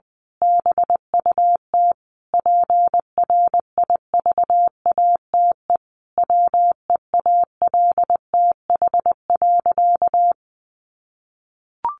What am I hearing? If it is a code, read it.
Morse code, heard as a keyed tone, about 20 wpm, 700 Hz. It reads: BUT PRIVATE WEALTH.
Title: Sentences from Top 1200 Words 20wpm